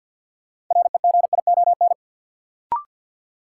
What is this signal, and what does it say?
Morse code, heard as a keyed tone, about 51 wpm, 700 Hz. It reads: REGION
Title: Sentences from Top 1500 Words - New Word Drill Alt - 50wpm